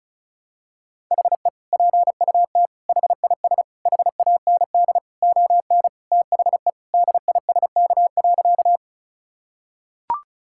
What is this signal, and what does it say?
Morse code, heard as a keyed tone, about 36 wpm, 700 Hz. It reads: HE PUT HIS HAND ON THE DISK.
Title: Sentences from Top 1000 Words - Review Part 6 - 35wpm